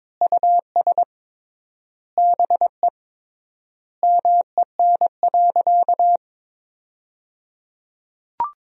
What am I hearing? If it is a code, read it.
Morse code, heard as a keyed tone, about 22 wpm, 700 Hz. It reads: US BE MEN.
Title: 3x Word-Spacing - Sentences from Top 300 Words - 22wpm